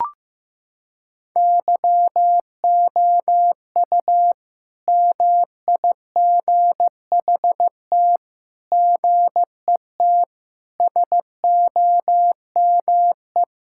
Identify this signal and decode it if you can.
Morse code, heard as a keyed tone, about 15 wpm, 700 Hz. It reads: YOU MIGHT GET SOME
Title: Sentences from Top 1000 Words - Review Part 5 - 15wpm